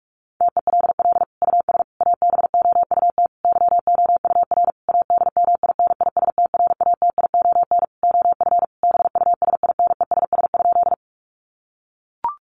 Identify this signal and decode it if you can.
Morse code, heard as a keyed tone, about 44 wpm, 700 Hz. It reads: TELL US ABOUT YOUR ADMINISTRATION OF BUSINESS?
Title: Sentences from Top 1300 Words 45wpm